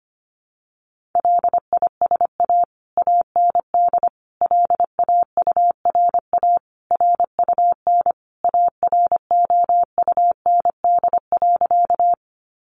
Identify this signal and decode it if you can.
Morse code, heard as a keyed tone, about 25 wpm, 700 Hz. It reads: LISA AND LAURA RUN AROUND.
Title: Additional Easy Phrases 25wpm